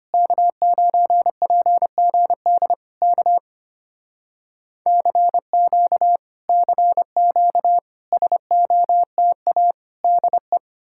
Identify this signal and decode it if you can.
Morse code, heard as a keyed tone, about 30 wpm, 700 Hz. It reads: K9PGD K CQ CQ SOTA DE
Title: Speed-Racing Common QSO Phrases - 25wpm